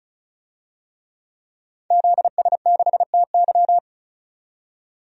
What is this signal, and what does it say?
Morse code, heard as a keyed tone, about 36 wpm, 700 Hz. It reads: ZS6TY
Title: DXCC-100 Call Signs - 5 Characters - 35wpm